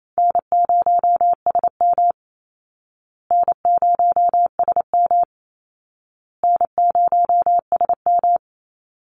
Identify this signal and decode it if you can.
Morse code, heard as a keyed tone, about 28 wpm, 700 Hz. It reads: N0SM N0SM N0SM
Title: Easy Call Signs Repeated 3x - 28wpm